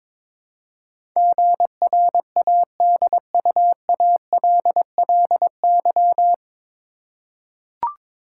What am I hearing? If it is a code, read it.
Morse code, heard as a keyed tone, about 22 wpm, 700 Hz. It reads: GRADUALLY
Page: Morse Code Ninja practice 22wpm